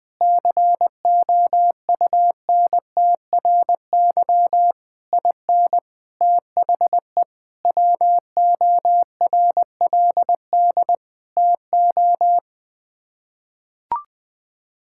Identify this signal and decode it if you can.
Morse code, heard as a keyed tone, about 21 wpm, 700 Hz. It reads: COUNTRY IN THE WORLD TO
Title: Sets of 8 Words - 20wpm